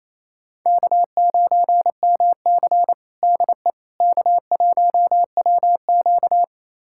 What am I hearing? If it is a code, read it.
Morse code, heard as a keyed tone, about 28 wpm, 700 Hz. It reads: K9MC DE K1WQ
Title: USA Callsign DE Callsign - 4 Characters - 28wpm